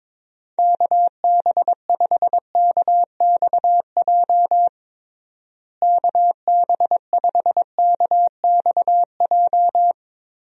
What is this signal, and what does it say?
Morse code, heard as a keyed tone, about 22 wpm, 700 Hz. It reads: KB5KXJ KB5KXJ
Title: USA Call Signs - 6 Characters - Repeated 3x - 22wpm